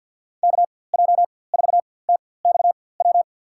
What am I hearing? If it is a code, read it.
Morse code, heard as a keyed tone, about 53 wpm, spaced as for 24 wpm, 700 Hz. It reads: KJ4TXW
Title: USA Call Signs - 4x Character Spacing - 50wpm